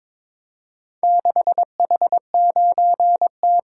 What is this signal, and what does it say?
Morse code, heard as a keyed tone, about 22 wpm, 700 Hz. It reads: 6H9T
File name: practice / 22wpm / Four Letter-Number Combinations - 22wpm